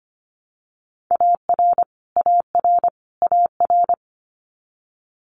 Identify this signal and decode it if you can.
Morse code, heard as a keyed tone, about 25 wpm, 700 Hz. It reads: AR AR AR
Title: Multi-letter Phonemes Repeated 3x 25wpm